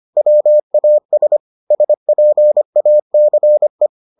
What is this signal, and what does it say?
Morse code, heard as a keyed tone, about 25 wpm, 600 Hz. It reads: WAS SPACE